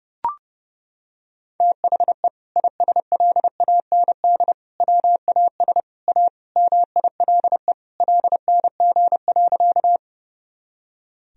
Morse code, heard as a keyed tone, about 32 wpm, 700 Hz. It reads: THE ISLAND WAS A MILE LNG.
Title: Sentences from Top 500 Words and Abbreviations 30wpm